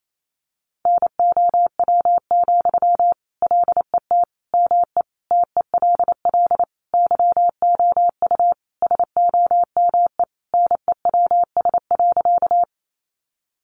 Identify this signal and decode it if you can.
Morse code, heard as a keyed tone, about 28 wpm, 700 Hz. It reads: NOW, LET ME TELL YOU SOME NEWS.
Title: Sentences from Top 800 Words 28wpm